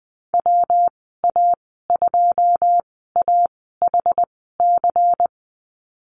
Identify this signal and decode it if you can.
Morse code, heard as a keyed tone, about 20 wpm, spaced as for 15 wpm, 700 Hz. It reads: WA2AHC
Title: 122 Lesson 20g - Number 3 - Call Signs 20wpm